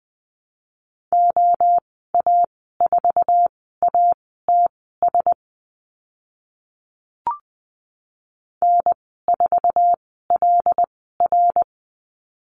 Morse code, heard as a keyed tone, about 20 wpm, spaced as for 14 wpm, 700 Hz. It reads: OA4ATS N4LR
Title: 076 Lesson 13g - Letter L - Call Signs 20wpm